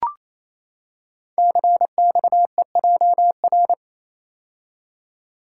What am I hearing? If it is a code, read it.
Morse code, heard as a keyed tone, about 28 wpm, 700 Hz. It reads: CXEJR
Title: Code Groups - 5 Letter:Numbers - 28wpm